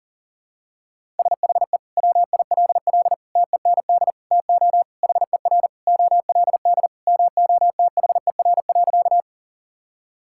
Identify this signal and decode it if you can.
Morse code, heard as a keyed tone, about 40 wpm, 700 Hz. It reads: SHE WILL TEND TO HER OLD MOTHER.
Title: Sentences from Top 1000 Words - Review Part 5 - 40wpm